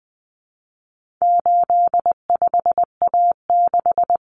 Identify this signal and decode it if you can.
Morse code, heard as a keyed tone, about 20 wpm, 700 Hz. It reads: 85A6
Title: Common Cut Numbers - 4-Digit 20wpm